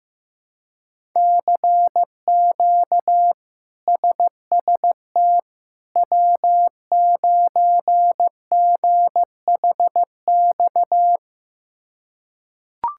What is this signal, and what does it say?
Morse code, heard as a keyed tone, about 15 wpm, 700 Hz. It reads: CQ SST W9GHX
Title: K1USN SST QSO Practice - 15wpm